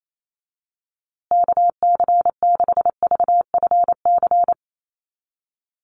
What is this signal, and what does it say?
Morse code, heard as a keyed tone, about 28 wpm, 700 Hz. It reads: KC6VFC
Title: USA Call Signs Repeated 2x 28wpm